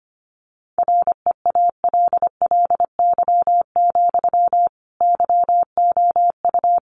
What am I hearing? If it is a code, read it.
Morse code, heard as a keyed tone, about 25 wpm, 700 Hz. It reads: REALLY, YOU
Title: Sentences from Top 1400 Words 25wpm